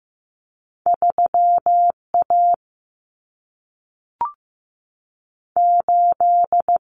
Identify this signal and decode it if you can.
Morse code, heard as a keyed tone, about 15 wpm, 700 Hz. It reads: 3A 8
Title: Common Cut Numbers - 2-Digit 15wpm